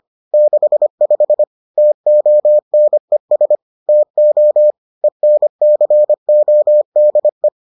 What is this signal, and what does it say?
Morse code, heard as a keyed tone, about 25 wpm, 600 Hz. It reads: 65 TONES TO ENCODE